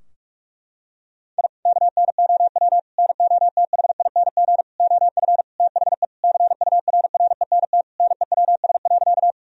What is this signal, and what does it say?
Morse code, heard as a keyed tone, about 46 wpm, 700 Hz. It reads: I KNOW NOTHING OF THE CURRENT NEWS.
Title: Sentences from Top 1000 Words - Review Part 4 - 45wpm